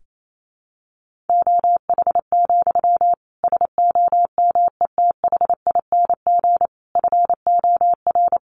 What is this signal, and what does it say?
Morse code, heard as a keyed tone, about 28 wpm, 700 Hz. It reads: OH, SOMETHING FOR
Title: Sentences from Top 1500 Words 28wpm